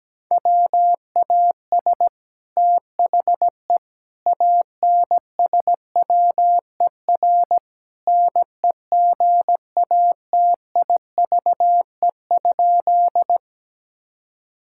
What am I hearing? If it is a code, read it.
Morse code, heard as a keyed tone, about 17 wpm, 700 Hz. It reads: WAS THE ANSWER NEGATIVE?
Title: Sentences from Top 1000 Words - Review Part 6 - 17wpm